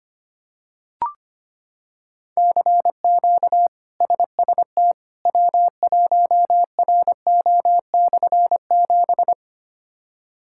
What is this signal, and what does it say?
Morse code, heard as a keyed tone, about 25 wpm, 700 Hz. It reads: CQ SST W1RO/7
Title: K1USN SST QSO Practice - 25wpm